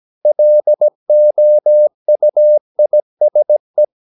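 Morse code, heard as a keyed tone, about 17 wpm, 600 Hz. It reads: LOUISE
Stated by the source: Slow (QRS) Morse practice at 17 wpm, from CW med Gnister.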